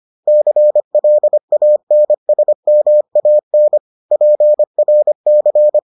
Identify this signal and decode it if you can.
Morse code, heard as a keyed tone, about 25 wpm, 600 Hz. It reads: CLANSMAN PRC